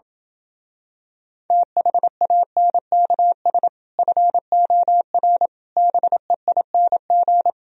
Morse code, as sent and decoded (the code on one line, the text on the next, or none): - .... .- -. -.- ... / ..-. --- .-. / -... .. -. --.
THANKS FOR BEING